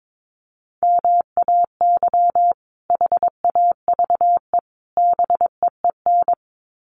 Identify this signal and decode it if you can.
Morse code, heard as a keyed tone, about 22 wpm, 700 Hz. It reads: MAY HAVE BEEN
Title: Sets of 3 Words 22wpm